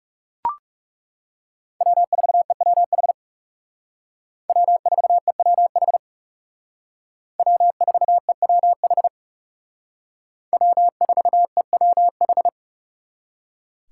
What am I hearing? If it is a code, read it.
Morse code, heard as a keyed tone, about 39 wpm, 700 Hz. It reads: W4EWH W4EWH W4EWH W4EWH